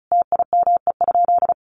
Morse code, heard as a keyed tone, about 34 wpm, 700 Hz. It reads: TIME?
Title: Sentences from Top 200 Words 35wpm